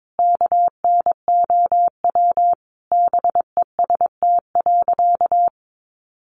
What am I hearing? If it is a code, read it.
Morse code, heard as a keyed tone, about 22 wpm, 700 Hz. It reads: KNOW BEST.
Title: Sentences from Top 600 Words 22wpm